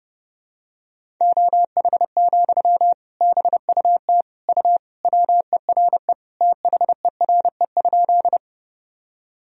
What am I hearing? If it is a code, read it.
Morse code, heard as a keyed tone, about 32 wpm, 700 Hz. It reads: OH, BUT U WERE THERE?
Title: Sentences from Top 500 Words and Abbreviations 30wpm